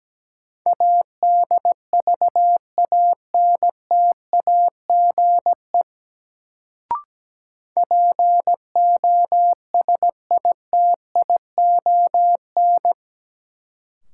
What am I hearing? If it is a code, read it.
Morse code, heard as a keyed tone, about 17 wpm, 700 Hz. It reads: ADVANTAGE POSITION
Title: Sentences from Top 900 Words - New Word Drill - 17wpm